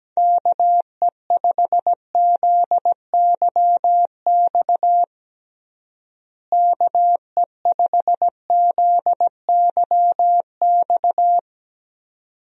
Morse code, heard as a keyed tone, about 17 wpm, 700 Hz. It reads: KE5ZYX KE5ZYX
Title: USA Call Signs - 6 Characters - Repeated 2x - 17wpm